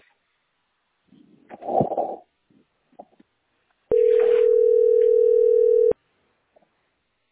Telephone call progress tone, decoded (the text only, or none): ringback tone